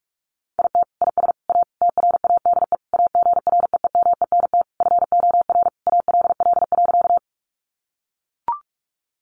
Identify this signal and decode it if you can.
Morse code, heard as a keyed tone, about 44 wpm, 700 Hz. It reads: IT IS A TRADE AGREEMENT FOR ALL.
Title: Sentences from Top 1500 Words 45wpm